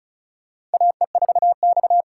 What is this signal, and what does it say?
Morse code, heard as a keyed tone, about 36 wpm, 700 Hz. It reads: AE4X
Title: USA Call Signs - 4 Characters - 3x Character Spacing - 35wpm